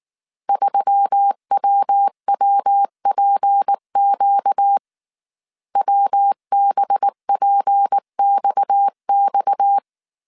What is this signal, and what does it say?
Morse code, heard as a keyed tone, about 19 wpm, 800 Hz. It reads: 3WWPQ WBPXX